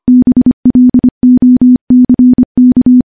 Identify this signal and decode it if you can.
Morse code, heard as a keyed tone, about 25 wpm, 250 Hz. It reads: BLOCK
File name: morse-training-25WPM